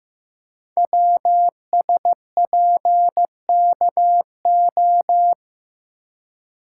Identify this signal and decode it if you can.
Morse code, heard as a keyed tone, about 15 wpm, 700 Hz. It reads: WSPKO
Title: Code Groups - 5 Letters - 15wpm